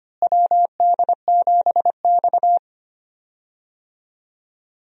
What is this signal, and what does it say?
Morse code, heard as a keyed tone, about 25 wpm, 700 Hz. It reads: WD7X
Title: USA Call Signs - 4 Characters - Repeated 2x - 25wpm